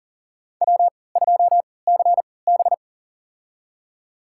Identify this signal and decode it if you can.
Morse code, heard as a keyed tone, about 40 wpm, spaced as for 23 wpm, 700 Hz. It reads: W2CB
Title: USA Call Signs - 4 Characters - 3x Character Spacing - 40wpm